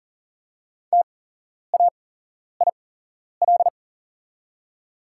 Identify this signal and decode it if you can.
Morse code, heard as a keyed tone, about 40 wpm, spaced as for 11 wpm, 690 Hz. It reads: TAIL
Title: 8x Character Spacing - Top 500 Words - 40wpm